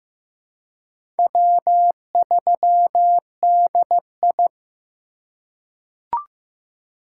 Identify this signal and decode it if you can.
Morse code, heard as a keyed tone, about 15 wpm, 700 Hz. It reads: W3DI
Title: USA Call Signs - 4 Characters - 2x Character Spacing - 15wpm